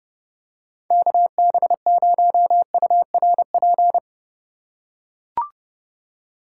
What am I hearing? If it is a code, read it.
Morse code, heard as a keyed tone, about 31 wpm, 700 Hz. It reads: KB0URP
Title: USA Call Signs - 6 Characters - 2x Character Spacing - 30wpm